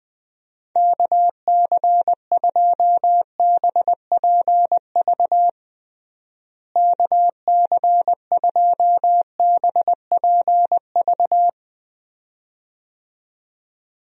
Morse code, heard as a keyed tone, about 20 wpm, 700 Hz. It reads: KC2BPV KC2BPV
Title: USA Call Signs - 6 Characters - Repeated 2x - 20wpm